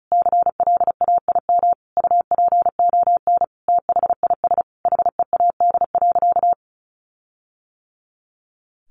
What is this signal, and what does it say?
Morse code, heard as a keyed tone, about 34 wpm, 700 Hz. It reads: CLAIM UPON THIS HEAD.
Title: Sentences from Top 1500 Words 35wpm